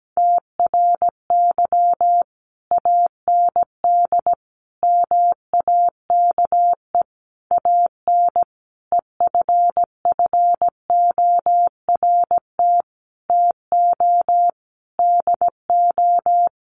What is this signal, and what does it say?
Morse code, heard as a keyed tone, about 17 wpm, 700 Hz. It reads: TRY AND MAKE AN EFFORT TO DO
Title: Sentences from Top 1200 Words 17wpm